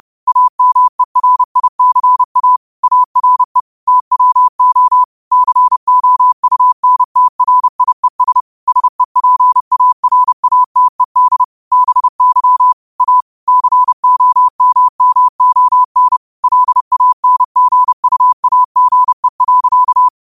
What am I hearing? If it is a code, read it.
Morse code, heard as a keyed tone, about 30 wpm, 1000 Hz. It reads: AMERICA ARE TWO COUNTRIES SEPARATED BY A COMMON LANGUAGE.